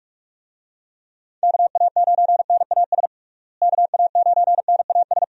Morse code, heard as a keyed tone, about 46 wpm, 700 Hz. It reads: KA9NAS KA9NAS
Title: USA Call Signs - 6 Characters - Repeated 2x - 45wpm